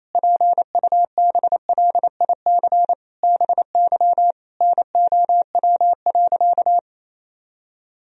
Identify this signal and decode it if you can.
Morse code, heard as a keyed tone, about 28 wpm, 700 Hz. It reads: PUBLIC BY NOW.
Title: Sentences from Top 1000 Words - Review Part 3 - 28wpm